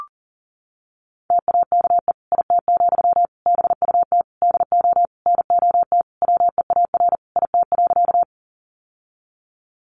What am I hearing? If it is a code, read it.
Morse code, heard as a keyed tone, about 37 wpm, 700 Hz. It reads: TAKE IT, BUT DO NOT WEAR IT.
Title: Sentences from Top 1500 Words 40wpm